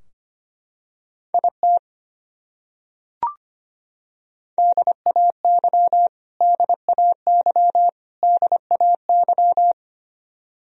Morse code, heard as a keyed tone, about 25 wpm, 700 Hz. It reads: IT DAY DAY DAY
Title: Top 100 Words Repeated 25wpm